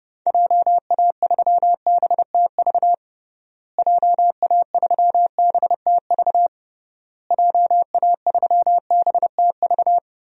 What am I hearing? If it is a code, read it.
Morse code, heard as a keyed tone, about 30 wpm, 700 Hz. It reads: JA3BTV JA3BTV JA3BTV
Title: DXCC-100 Call Signs - 6 Characters - Repeated 3x - 30wpm